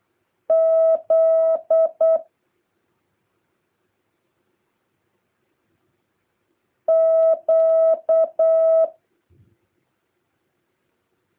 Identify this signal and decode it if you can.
Morse code, heard as a keyed tone, about 8 wpm, 650 Hz. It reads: Z Q
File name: jh8cbh / 2023cw